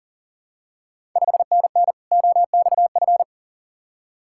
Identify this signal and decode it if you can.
Morse code, heard as a keyed tone, about 40 wpm, 700 Hz. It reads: 5NN OXF
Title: Ontario QSO Party Exchange 40wpm